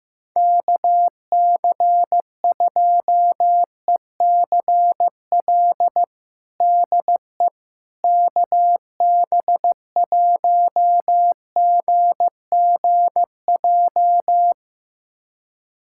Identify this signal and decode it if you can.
Morse code, heard as a keyed tone, about 16 wpm, 700 Hz. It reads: KC2ECL DE KB1GGJ
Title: USA Callsign DE Callsign - 6 Characters - 15wpm